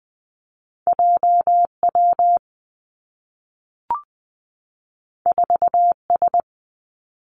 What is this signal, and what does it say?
Morse code, heard as a keyed tone, about 20 wpm, 700 Hz. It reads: J W 4 S
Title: Two Challenging Letter-Number Combinations 20wpm